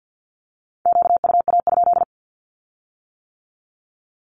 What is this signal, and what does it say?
Morse code, heard as a keyed tone, about 50 wpm, 700 Hz. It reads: QUA?
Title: Q-Codes with Spoken Meaning - 50wpm